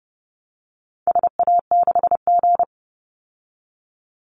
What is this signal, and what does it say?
Morse code, heard as a keyed tone, about 30 wpm, 700 Hz. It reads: SA6G
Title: Call Signs - 30wpm